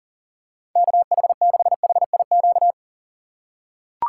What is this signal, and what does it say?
Morse code, heard as a keyed tone, about 40 wpm, 700 Hz. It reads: KH6HIQ
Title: USA Call Signs - 3x Character Spacing - 40wpm